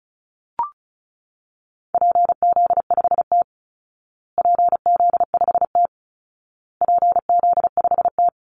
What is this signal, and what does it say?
Morse code, heard as a keyed tone, about 34 wpm, 700 Hz. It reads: PZ5T PZ5T PZ5T
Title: Call Signs Repeated 2x - 35wpm